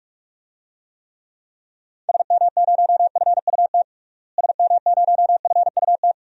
Morse code, heard as a keyed tone, about 43 wpm, 700 Hz. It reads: SM0FUT SM0FUT
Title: DXCC-100 Call Signs - 6 Characters - Repeated 2x - 45wpm